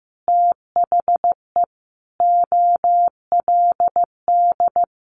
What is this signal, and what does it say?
Morse code, heard as a keyed tone, about 15 wpm, 700 Hz. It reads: THE OLD
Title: Sentences from Top 1300 Words 15wpm